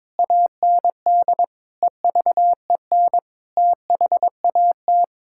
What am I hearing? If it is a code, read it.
Morse code, heard as a keyed tone, about 22 wpm, 700 Hz. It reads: AND EVEN THAT
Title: Sentences from Top 1600 Words 22wpm